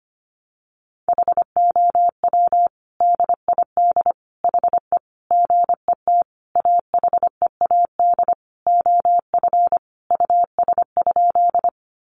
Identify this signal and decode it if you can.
Morse code, heard as a keyed tone, about 25 wpm, 700 Hz. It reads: HOW DID HE GET AHEAD OF US?
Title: Sentences from Top 1100 Words 25wpm